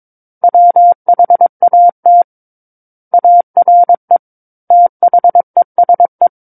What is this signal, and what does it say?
Morse code, heard as a keyed tone, about 22 wpm, 700 Hz. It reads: WHAT ARE THESE
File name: Book One / 13-Chapter 13 - 22 WPM